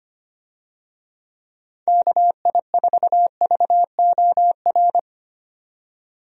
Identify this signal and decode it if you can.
Morse code, heard as a keyed tone, about 25 wpm, 700 Hz. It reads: KI4VOR
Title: USA Call Signs - 6 Characters - 25wpm